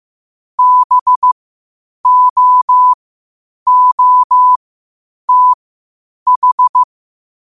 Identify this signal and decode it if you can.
Morse code, heard as a keyed tone, about 15 wpm, spaced as for 8 wpm, 1000 Hz. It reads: BOOTH
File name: qotdfiles / 8wpm